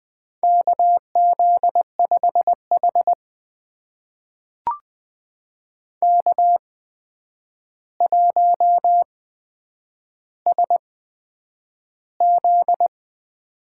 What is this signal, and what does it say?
Morse code, heard as a keyed tone, about 20 wpm, spaced as for 9 wpm, 700 Hz. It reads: KZ5H K1SZ
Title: USA Call Signs - 4 Characters - 8x Character Spacing - 20wpm